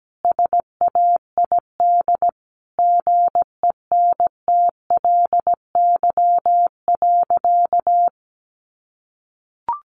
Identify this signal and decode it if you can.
Morse code, heard as a keyed tone, about 17 wpm, 700 Hz. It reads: SAID GENTLY.
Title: Sentences from Top 1500 Words 17wpm